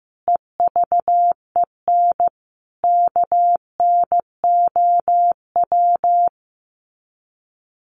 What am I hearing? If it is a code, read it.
Morse code, heard as a keyed tone, about 15 wpm, 700 Hz. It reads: EVEN KNOW
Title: Sets of 5 Words 15wpm